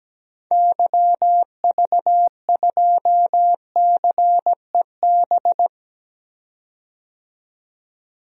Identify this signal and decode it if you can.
Morse code, heard as a keyed tone, about 17 wpm, 700 Hz. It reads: YV2CEB
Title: DXCC-100 Call Signs - 6 Characters - 17wpm